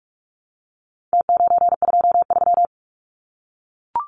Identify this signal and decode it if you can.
Morse code, heard as a keyed tone, about 44 wpm, 700 Hz. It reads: T923
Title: Numbers - Contest Style - 4-Digit 45wpm